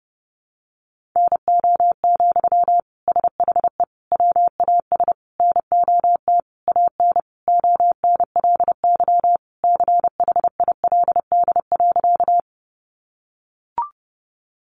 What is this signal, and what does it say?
Morse code, heard as a keyed tone, about 30 wpm, 700 Hz. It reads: NO, SHE WAS NOT AN ONLY CHILD.
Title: Sentences from Top 800 Words 30wpm